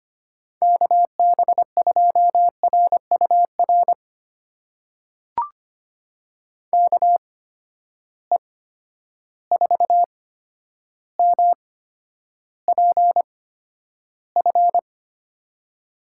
Morse code, heard as a keyed tone, about 25 wpm, spaced as for 11 wpm, 700 Hz. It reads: KB2RUR KE4MPF
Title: USA Call Signs - 6 Characters - 8x Character Spacing - 25wpm